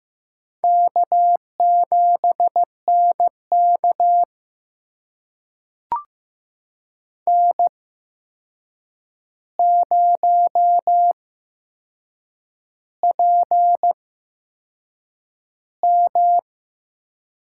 Morse code, heard as a keyed tone, about 15 wpm, spaced as for 6 wpm, 700 Hz. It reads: K7NK N0PM
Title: USA Call Signs - 4 Characters - 8x Character Spacing - 15wpm